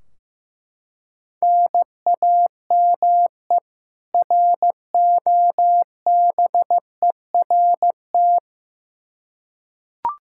Morse code, heard as a keyed tone, about 15 wpm, 700 Hz. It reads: NAME ROBERT
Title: Practice QSO - Exchange 1 - Beginning - 15wpm